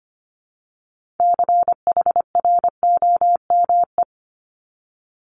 Morse code, heard as a keyed tone, about 25 wpm, 700 Hz. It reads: CHROME
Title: Multi-letter Phonemes Repeated 3x 25wpm